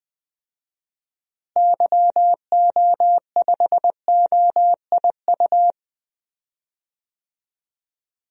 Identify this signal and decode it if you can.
Morse code, heard as a keyed tone, about 20 wpm, 700 Hz. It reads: YO5OIU
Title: DXCC-100 Call Signs - 6 Characters - Repeated 3x - 20wpm